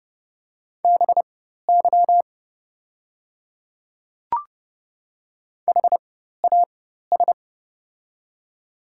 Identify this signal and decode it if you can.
Morse code, heard as a keyed tone, about 30 wpm, spaced as for 14 wpm, 700 Hz. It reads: BY HAS
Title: 4x Character Spacing - Top 100 Words - 30wpm